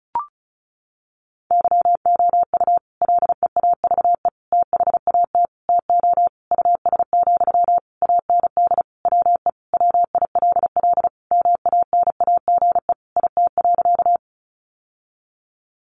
Morse code, heard as a keyed tone, about 34 wpm, 700 Hz. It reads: YOU LEAVE THAT TO US, AND WE WILL MANAGE IT.
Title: Sentences from Top 1400 Words 35wpm